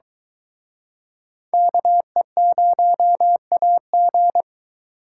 Morse code, heard as a keyed tone, about 23 wpm, 700 Hz. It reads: KE0AG